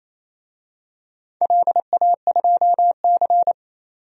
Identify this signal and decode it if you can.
Morse code, heard as a keyed tone, about 28 wpm, 700 Hz. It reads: LA2C
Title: DXCC-100 Call Signs - 4 Characters - 28wpm